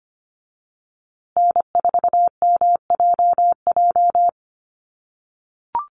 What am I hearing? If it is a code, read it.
Morse code, heard as a keyed tone, about 25 wpm, 700 Hz. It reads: N4MJJ
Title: USA Call Signs Repeated 3x 25wpm